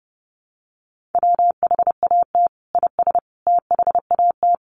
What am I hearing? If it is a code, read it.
Morse code, heard as a keyed tone, about 30 wpm, 700 Hz. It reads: WHAT IS THAT
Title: Sentences from Top 1500 Words 30wpm